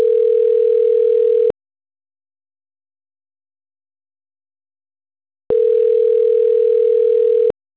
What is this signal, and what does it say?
Ringback tone (North America), 440 + 480 Hz.
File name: autotestagent / 10-41-01.136..callcentertransfer. .cc9b6b85-13ed-4ac9-916f